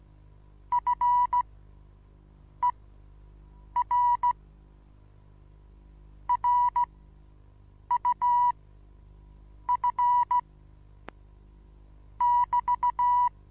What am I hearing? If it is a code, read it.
Morse code, heard as a keyed tone, about 16 wpm, spaced as for 6 wpm, 980 Hz. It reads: FER RUF <BT>
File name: Onlinelehrgang / morsen / DJ4UF Morsen25